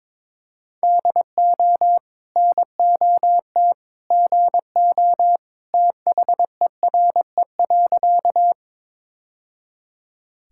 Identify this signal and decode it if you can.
Morse code, heard as a keyed tone, about 22 wpm, 700 Hz. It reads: DO NOT GO THERE.
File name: practice / 22wpm / Sentences with 2-Word Phrases - 22wpm